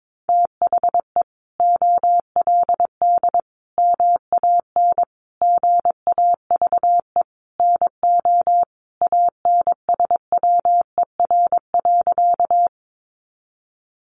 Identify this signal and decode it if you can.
Morse code, heard as a keyed tone, about 22 wpm, 700 Hz. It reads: THE OLD MAN GAVE NO ANSWER.
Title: Sentences from Top 500 Words 22wpm